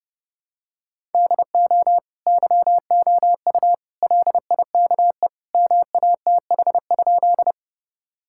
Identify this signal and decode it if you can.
Morse code, heard as a keyed tone, about 30 wpm, 700 Hz. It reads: DO YOU LIKE MATH?
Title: Sentences from Top 1000 Words - Review Part 6 - 30wpm